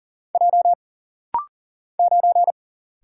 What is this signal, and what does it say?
Morse code, heard as a keyed tone, about 40 wpm, 700 Hz. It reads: J 9
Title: Single Letter-Number Alt - Rapid-Fire 40wpm